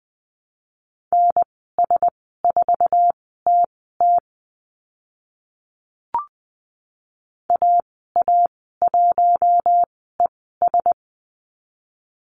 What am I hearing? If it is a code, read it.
Morse code, heard as a keyed tone, about 20 wpm, spaced as for 14 wpm, 700 Hz. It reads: NS4TT AA1ES
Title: 055 Lesson 10g - Letter r - Call Signs 20wpm